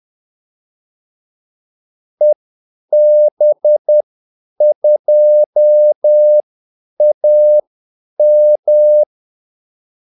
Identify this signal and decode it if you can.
Morse code, heard as a keyed tone, about 10 wpm, spaced as for 8 wpm, 600 Hz. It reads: EB2AM
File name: Callsigns-10-8wpm